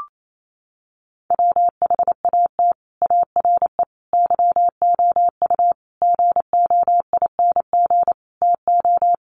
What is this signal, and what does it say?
Morse code, heard as a keyed tone, about 28 wpm, 700 Hz. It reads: WHAT ARE YOU GOING TO
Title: Sets of 5 Words 28wpm